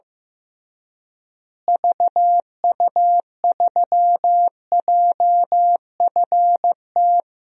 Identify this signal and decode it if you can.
Morse code, heard as a keyed tone, about 16 wpm, 700 Hz. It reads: VU3JFT
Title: DXCC-100 Call Signs - 6 Characters - Repeated 3x - 15wpm